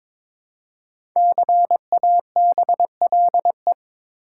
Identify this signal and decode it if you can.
Morse code, heard as a keyed tone, about 22 wpm, 700 Hz. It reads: CABLE
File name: 5 Letter Words - First 3 Letters Given - 22wpm